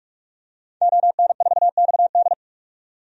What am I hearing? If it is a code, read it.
Morse code, heard as a keyed tone, about 47 wpm, 700 Hz. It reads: ON4XD
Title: DXCC-100 Call Signs - 5 Characters - 45wpm